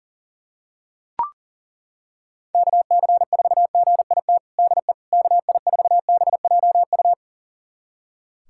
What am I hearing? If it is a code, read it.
Morse code, heard as a keyed tone, about 40 wpm, 700 Hz. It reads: KC4GIT DE KI4BJU
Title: USA Callsign DE Callsign 40wpm